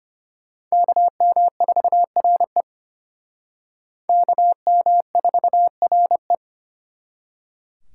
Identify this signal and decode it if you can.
Morse code, heard as a keyed tone, about 32 wpm, 700 Hz. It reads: KM4RE KM4RE